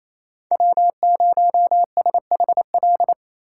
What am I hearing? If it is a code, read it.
Morse code, heard as a keyed tone, about 28 wpm, 700 Hz. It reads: W0SHL
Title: K1USN SST QSO Practice - 28wpm